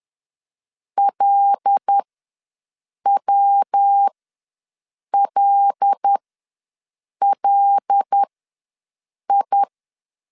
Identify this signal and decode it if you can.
Morse code, heard as a keyed tone, about 10 wpm, spaced as for 6 wpm, 800 Hz. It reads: LWLLI
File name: LEZIONI / 5 LBVC / 5 L ws 10wpm 3636